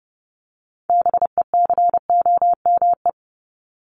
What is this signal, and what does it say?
Morse code, heard as a keyed tone, about 30 wpm, 700 Hz. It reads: BECOME
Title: Top 501-750 Words 30wpm